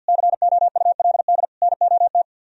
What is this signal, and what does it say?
Morse code, heard as a keyed tone, about 62 wpm, 700 Hz. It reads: COULD NOT
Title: Speed-Racing - Sets of 3 Words 50wpm